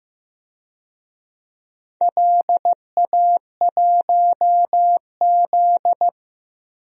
Morse code, heard as a keyed tone, about 15 wpm, 700 Hz. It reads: LA1Z
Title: DXCC-100 Call Signs - 4 Characters - 15wpm